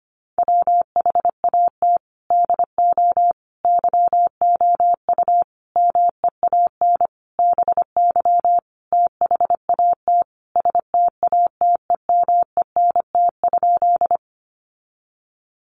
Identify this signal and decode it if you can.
Morse code, heard as a keyed tone, about 25 wpm, 700 Hz. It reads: WHAT DO YOU MEAN BY THAT STATEMENT?
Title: Sentences from Top 1200 Words 25wpm